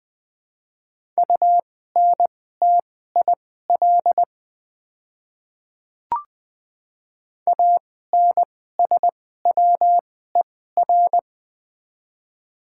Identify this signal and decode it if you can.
Morse code, heard as a keyed tone, about 21 wpm, spaced as for 15 wpm, 700 Hz. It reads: UNTIL ANSWER
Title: 2x Character Spacing - Top 300 Words - 20wpm